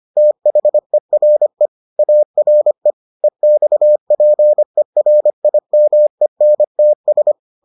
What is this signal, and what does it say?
Morse code, heard as a keyed tone, about 25 wpm, 600 Hz. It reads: THERE ARE EXPERIMENTS